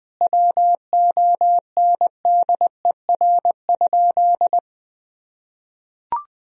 Morse code, heard as a keyed tone, about 20 wpm, 700 Hz. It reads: WONDER?
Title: Sentences from Top 1000 Words - Review Part 3 - 20wpm